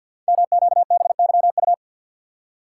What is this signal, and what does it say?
Morse code, heard as a keyed tone, about 52 wpm, 690 Hz. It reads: MQBYU
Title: Code Groups - 5 Letters - 50wpm